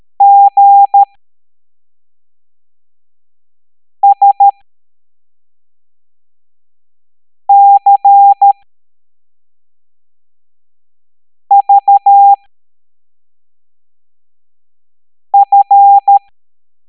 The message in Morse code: GSCVF